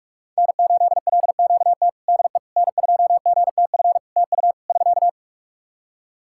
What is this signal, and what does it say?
Morse code, heard as a keyed tone, about 46 wpm, 700 Hz. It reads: N8LQT DE N2GTF TU <SK>
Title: K1USN SST QSO Practice 45wpm